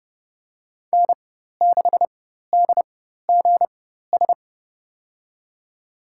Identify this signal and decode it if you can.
Morse code, heard as a keyed tone, about 30 wpm, spaced as for 14 wpm, 700 Hz. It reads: N6DGS
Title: USA Call Signs - 4x Character Spacing - 30wpm